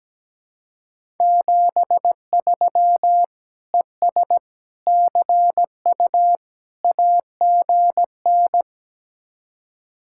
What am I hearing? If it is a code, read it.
Morse code, heard as a keyed tone, about 17 wpm, 700 Hz. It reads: 73 ES CU AGN
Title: Speed-Racing Common QSO Phrases - 17wpm